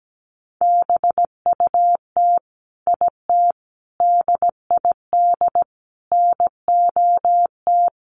Morse code, heard as a keyed tone, about 17 wpm, 700 Hz. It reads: BUT IT DID NOT
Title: Sentences from Top 600 Words 17wpm